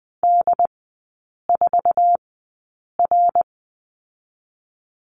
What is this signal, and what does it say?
Morse code, heard as a keyed tone, about 20 wpm, spaced as for 8 wpm, 700 Hz. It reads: D4R
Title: 145 Lesson 24e - Letter y - Three Characters 20wpm